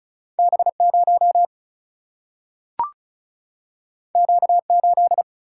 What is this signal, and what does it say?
Morse code, heard as a keyed tone, about 36 wpm, 700 Hz. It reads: B 0 Q 8
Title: Single-Letter Single-Number 35wpm